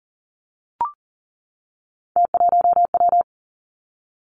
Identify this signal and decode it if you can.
Morse code, heard as a keyed tone, about 39 wpm, 700 Hz. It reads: T1W
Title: Three Challenging Letter-Number Combinations 40wpm